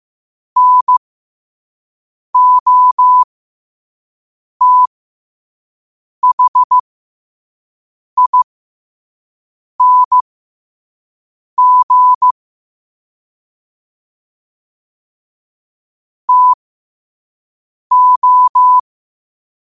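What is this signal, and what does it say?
Morse code, heard as a keyed tone, about 15 wpm, spaced as for 5 wpm, 1000 Hz. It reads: NOTHING TO
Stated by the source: Quote-of-the-day Morse code practice at 5 wpm.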